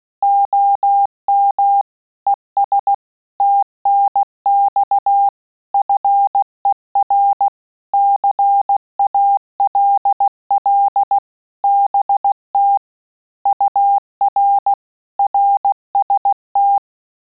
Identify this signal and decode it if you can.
Morse code, heard as a keyed tone, about 16 wpm, 800 Hz. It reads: OM ES TNX FER CALL BT UR RST
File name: QSO CW 75 CPM